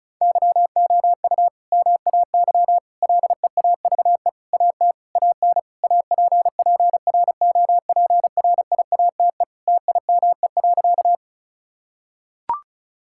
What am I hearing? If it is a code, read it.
Morse code, heard as a keyed tone, about 34 wpm, 700 Hz. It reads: YOU MAY LEAVE AT AN APPROPRIATE TIME.
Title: Sentences from Top 1000 Words - Review Part 5 - 35wpm